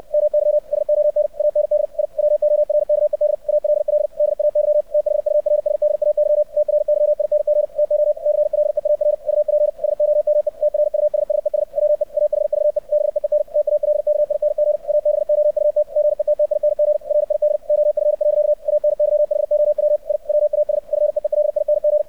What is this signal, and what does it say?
Morse code, heard as a keyed tone, about 64 wpm, 600 Hz. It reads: MY SON AND I WORKED ALL DAY THURSDAY TRYING TO CLEAR UP SOME TRUSSES WE AHVE BEEN TALKING ABOUT GETTING RID OF. ANYHOW I WAS FEELING
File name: archived / CRSnet